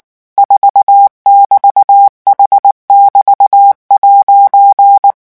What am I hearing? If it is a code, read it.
Morse code, heard as a keyed tone, about 19 wpm, 800 Hz. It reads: 4<BT>H<BT>'
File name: morse mp3 / revision14-vitesse 18 mots